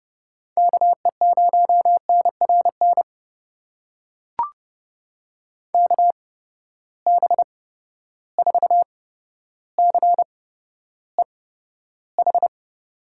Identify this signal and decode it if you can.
Morse code, heard as a keyed tone, about 32 wpm, spaced as for 13 wpm, 700 Hz. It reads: KE0NRN KB4CEH
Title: USA Call Signs - 6 Characters - 8x Character Spacing - 30wpm